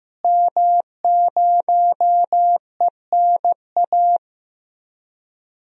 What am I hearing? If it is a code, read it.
Morse code, heard as a keyed tone, about 15 wpm, 700 Hz. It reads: M0ENA
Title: DXCC-100 Call Signs - 5 Characters - Repeated 3x - 15wpm